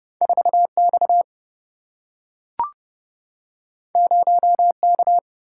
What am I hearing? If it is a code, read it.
Morse code, heard as a keyed tone, about 30 wpm, 700 Hz. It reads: 4X 0K
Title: Single-Letter Single-Number 30wpm